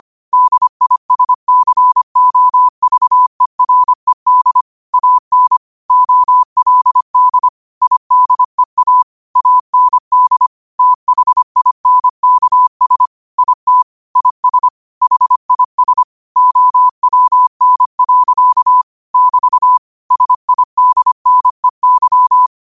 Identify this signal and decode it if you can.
Morse code, heard as a keyed tone, about 25 wpm, 1000 Hz. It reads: DISCOVERED AN OLD IDEA AND THINKS IT IS HIS OWN. <BT> SIDNEY
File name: qotdfiles / 25wpm